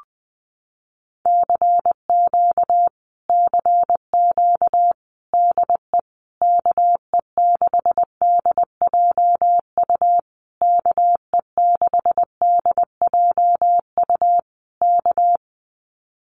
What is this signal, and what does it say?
Morse code, heard as a keyed tone, about 20 wpm, 700 Hz. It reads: CQ CQ DE KE6DJU KE6DJU K
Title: Common QSO Phrases Encore 20wpm